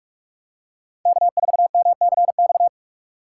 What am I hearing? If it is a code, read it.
Morse code, heard as a keyed tone, about 49 wpm, 700 Hz. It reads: K4MCX